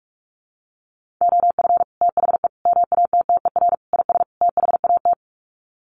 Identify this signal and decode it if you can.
Morse code, heard as a keyed tone, about 44 wpm, 700 Hz. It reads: OF THE MATTER IS THAT
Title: Sets of 5 Words Encore 45wpm